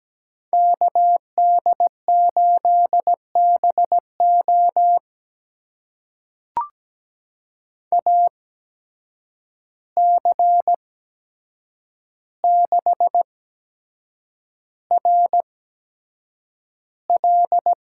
Morse code, heard as a keyed tone, about 17 wpm, spaced as for 7 wpm, 700 Hz. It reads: KD8BO AC6RL
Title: USA Call Signs - 5 Characters - 8x Character Spacing - 17wpm